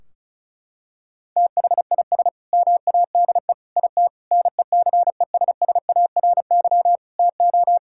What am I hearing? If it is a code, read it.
Morse code, heard as a keyed tone, about 34 wpm, 700 Hz. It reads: THIS MADE IT NECESSARY TO
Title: Sentences from Top 1600 Words 35wpm